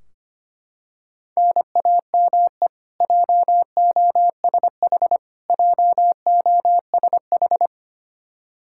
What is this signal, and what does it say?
Morse code, heard as a keyed tone, about 25 wpm, 700 Hz. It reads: NAME JOSH JOSH
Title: K1USN SST QSO Practice 25wpm